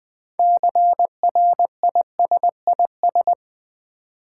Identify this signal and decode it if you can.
Morse code, heard as a keyed tone, about 20 wpm, 700 Hz. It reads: CRISIS
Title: Build A Word - 3 Letters At A Time - 6-Letter Words - 20wpm